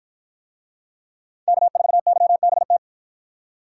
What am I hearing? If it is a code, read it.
Morse code, heard as a keyed tone, about 53 wpm, 700 Hz. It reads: K4YBT